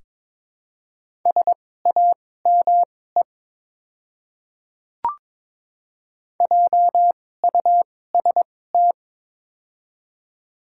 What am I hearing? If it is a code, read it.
Morse code, heard as a keyed tone, about 22 wpm, spaced as for 16 wpm, 700 Hz. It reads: SAME JUST
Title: 2x Character Spacing - Top 200 Words - 22wpm